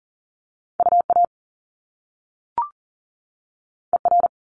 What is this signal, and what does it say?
Morse code, heard as a keyed tone, about 42 wpm, 700 Hz. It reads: UA ER
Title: Top Two Letter Combinations in Words 40wpm